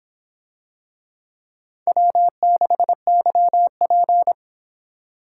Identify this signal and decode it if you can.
Morse code, heard as a keyed tone, about 24 wpm, 700 Hz. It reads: W6YP